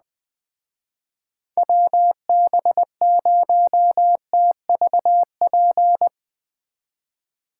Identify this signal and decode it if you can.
Morse code, heard as a keyed tone, about 20 wpm, 700 Hz. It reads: WB0TVP
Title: Speed Racing - USA 2x3 Call Signs 17wpm